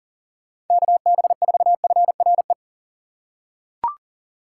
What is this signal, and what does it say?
Morse code, heard as a keyed tone, about 37 wpm, 700 Hz. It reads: KB4FRE
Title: USA Call Signs - 6 Characters - 6x Character Spacing - 40wpm